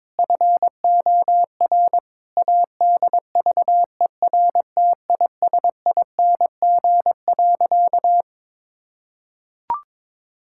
Morse code, heard as a keyed tone, about 22 wpm, 700 Hz. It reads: FOR ADVERTISING.
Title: Sentences from Top 1000 Words - Review Part 6 - 22wpm